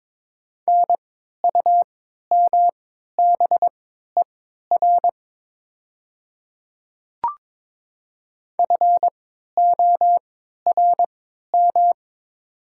Morse code, heard as a keyed tone, about 22 wpm, spaced as for 12 wpm, 700 Hz. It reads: NUMBER FORM
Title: 3x Character Spacing - Top 300 Words - 22wpm